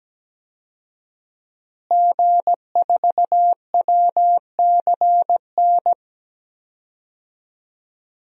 Morse code, heard as a keyed tone, about 17 wpm, 700 Hz. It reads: G4WCN
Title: DXCC-100 Call Signs - 5 Characters - Repeated 2x - 17wpm